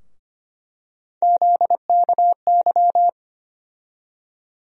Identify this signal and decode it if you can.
Morse code, heard as a keyed tone, about 25 wpm, 700 Hz. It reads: ZKY
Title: USA Call Sign Postfixes - 3 Letters 25wpm